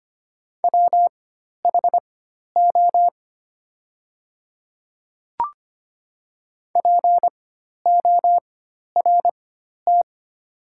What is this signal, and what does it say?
Morse code, heard as a keyed tone, about 25 wpm, spaced as for 12 wpm, 700 Hz. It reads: WHO PORT
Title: 4x Character Spacing - Top 200 Words - 25wpm